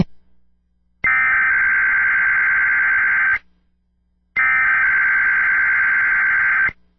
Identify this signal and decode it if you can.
Emergency Alert System (SAME) tones (header bursts), heard as harsh buzzing bursts. It is heard from 1.1 to 6.4 s.